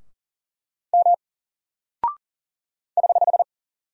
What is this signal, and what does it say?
Morse code, heard as a keyed tone, about 40 wpm, 700 Hz. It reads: M <HH>
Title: Single Character (Letter:Number:Punctuation:Prosign) Alt 40wpm